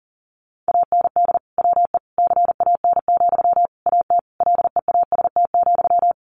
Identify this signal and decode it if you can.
Morse code, heard as a keyed tone, about 40 wpm, 700 Hz. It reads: AND WE CAN, AT LEAST,
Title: Sentences from Top 900 Words 40wpm